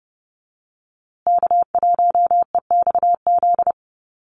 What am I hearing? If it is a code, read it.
Morse code, heard as a keyed tone, about 30 wpm, 700 Hz. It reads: K1EXZ
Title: USA Call Signs Repeated 2x 30wpm